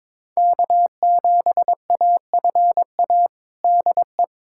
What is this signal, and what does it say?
Morse code, heard as a keyed tone, about 21 wpm, 700 Hz. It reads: K7AFA DE
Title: USA Callsign DE Callsign - 5 Characters - 22wpm